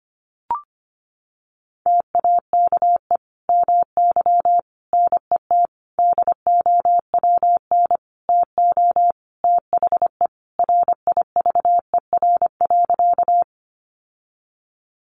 Morse code, heard as a keyed tone, about 25 wpm, 700 Hz. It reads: TAKE MY NET DOWN TO THE RIVER.
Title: Sentences from Top 1400 Words 25wpm